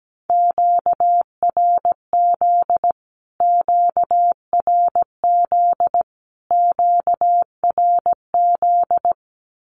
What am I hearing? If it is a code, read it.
Morse code, heard as a keyed tone, about 17 wpm, 700 Hz. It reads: QRZ QRZ QRZ
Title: Q-Codes Repeated 3x 17wpm